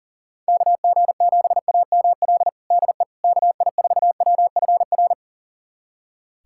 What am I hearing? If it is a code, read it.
Morse code, heard as a keyed tone, about 40 wpm, 700 Hz. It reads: KG7AML DE KI4WFR
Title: USA Callsign DE Callsign 40wpm